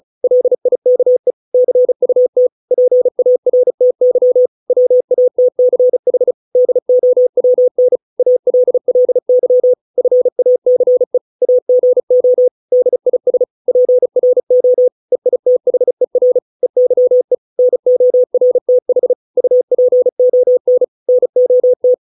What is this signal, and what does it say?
Morse code, heard as a keyed tone, about 35 wpm, 500 Hz. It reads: LIKE CUT PARTY WATCH DOWN ALLY FACE AGO DIS PRO EITHER EYE NORTH UPON NOT